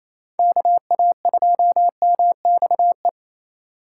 Morse code, heard as a keyed tone, about 27 wpm, 700 Hz. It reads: KA2MXE
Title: USA Call Signs - 6 Characters - 2x Character Spacing - 28wpm